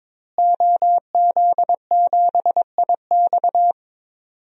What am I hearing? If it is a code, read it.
Morse code, heard as a keyed tone, about 22 wpm, 700 Hz. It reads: OZ7IX
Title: DXCC-100 Call Signs - 5 Characters - Repeated 3x - 22wpm